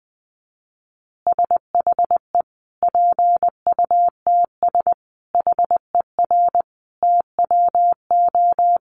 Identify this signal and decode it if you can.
Morse code, heard as a keyed tone, about 20 wpm, 700 Hz. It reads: SHE PUTS HER TWO
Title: Easy Phrases 20wpm